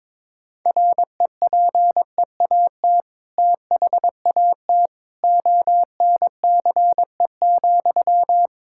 Morse code, heard as a keyed tone, about 22 wpm, 700 Hz. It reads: REPEAT THAT ONCE,
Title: Sentences from Top 1600 Words 22wpm